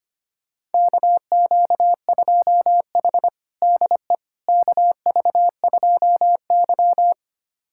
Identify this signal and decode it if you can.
Morse code, heard as a keyed tone, about 25 wpm, 700 Hz. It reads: KQ2H DE KV2Y
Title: USA Callsign DE Callsign - 4 Characters - 25wpm